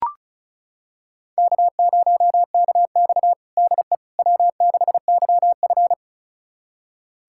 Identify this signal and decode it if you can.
Morse code, heard as a keyed tone, about 36 wpm, 700 Hz. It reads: K0KX DE W6YF
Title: USA Callsign DE Callsign - 4 Characters - 35wpm